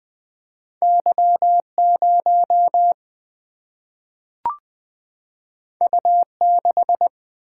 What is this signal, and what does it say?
Morse code, heard as a keyed tone, about 20 wpm, 700 Hz. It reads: Y0 U6
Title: Single-Letter Single-Number 20wpm